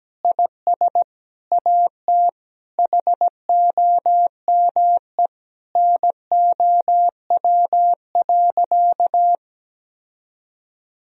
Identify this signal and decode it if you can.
Morse code, heard as a keyed tone, about 17 wpm, 700 Hz. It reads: IS AT HOME NOW.
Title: Sentences from Top 1000 Words - Review Part 3 - 17wpm